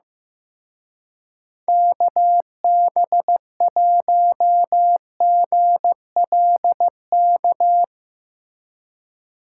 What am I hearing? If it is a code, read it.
Morse code, heard as a keyed tone, about 15 wpm, 700 Hz. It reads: KB1GLK
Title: Speed Racing - USA 2x3 Call Signs 15wpm